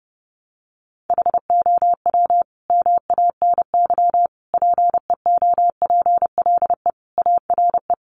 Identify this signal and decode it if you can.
Morse code, heard as a keyed tone, about 30 wpm, 700 Hz. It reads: HOW MANY PEOPLE ARE
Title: Sentences from Top 1200 Words 30wpm